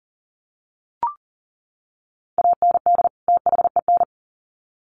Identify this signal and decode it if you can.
Morse code, heard as a keyed tone, about 40 wpm, 700 Hz. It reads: AND THEN
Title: Sets of 2 Words 40wpm